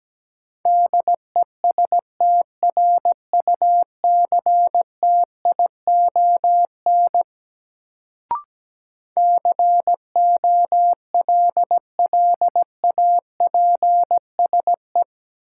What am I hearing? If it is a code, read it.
Morse code, heard as a keyed tone, about 17 wpm, 700 Hz. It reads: DESTRUCTION COLLAPSE
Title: Sets of 5 Words Encore - New Word Drill - 17wpm